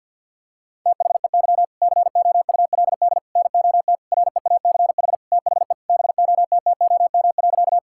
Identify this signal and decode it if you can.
Morse code, heard as a keyed tone, about 50 wpm, 700 Hz. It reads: THEY COULD NOT REACH THE BOTTOM.
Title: Sentences from Top 1000 Words - Review Part 6 - 50wpm